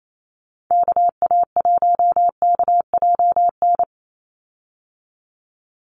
Morse code, heard as a keyed tone, about 28 wpm, 700 Hz. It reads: KA1KJN
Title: USA Call Signs 28wpm